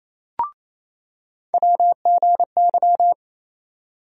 Morse code, heard as a keyed tone, about 28 wpm, 700 Hz. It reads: WGY
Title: USA Call Sign Postfixes - 3 Letters 28wpm